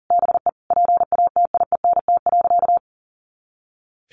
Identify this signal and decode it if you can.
Morse code, heard as a keyed tone, about 40 wpm, 700 Hz. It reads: BE PATIENT.
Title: Easy Phrases 40wpm